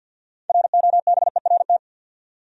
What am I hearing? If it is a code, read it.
Morse code, heard as a keyed tone, about 52 wpm, 700 Hz. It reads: ROBERT
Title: USA Common Names - 50wpm